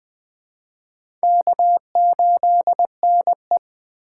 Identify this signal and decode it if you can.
Morse code, heard as a keyed tone, about 20 wpm, 700 Hz. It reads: K8NE
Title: USA Call Signs - 4 Characters - 4x Character Spacing - 20wpm